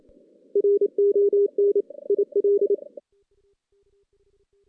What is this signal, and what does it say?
Morse code, heard as a keyed tone, about 26 wpm, 410 Hz. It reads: RON IL
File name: 2016 JAN CW NAQP / 2016 Jan naqp k5ot - 0-5